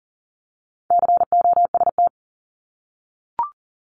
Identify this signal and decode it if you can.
Morse code, heard as a keyed tone, about 38 wpm, 700 Hz. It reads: COST
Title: Multi-letter Phonemes Repeated 3x 40wpm